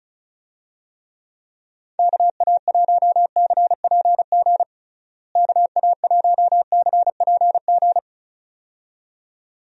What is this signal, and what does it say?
Morse code, heard as a keyed tone, about 36 wpm, 700 Hz. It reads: KA1CPG KA1CPG
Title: USA Call Signs - 6 Characters - Repeated 2x - 35wpm